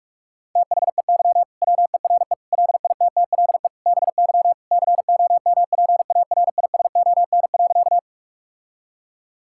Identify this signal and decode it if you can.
Morse code, heard as a keyed tone, about 42 wpm, 700 Hz. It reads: THEY WERE LITTLE BY COMPARISON.
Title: Sentences from Top 1600 Words 45wpm